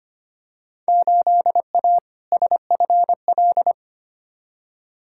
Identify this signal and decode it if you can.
Morse code, heard as a keyed tone, about 25 wpm, 700 Hz. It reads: 8A SFL
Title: Field Day Exchanges Encore - 25wpm